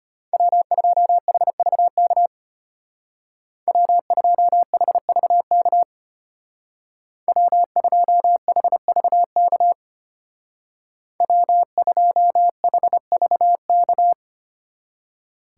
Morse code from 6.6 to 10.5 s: W2HVK

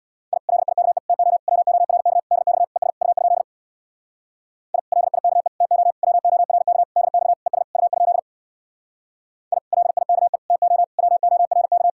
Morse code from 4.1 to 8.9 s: I LIKE TO WORK DX ES U?

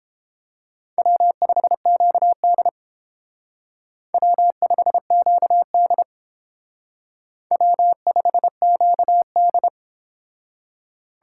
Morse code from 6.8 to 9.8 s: W5QD

